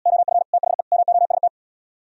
Morse code, heard as a keyed tone, about 77 wpm, 700 Hz. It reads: OF THE MOST